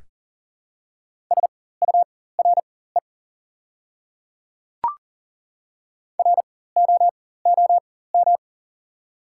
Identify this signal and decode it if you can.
Morse code, heard as a keyed tone, about 40 wpm, spaced as for 19 wpm, 700 Hz. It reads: SURE ROOM
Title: 4x Character Spacing - Top 300 Words - 40wpm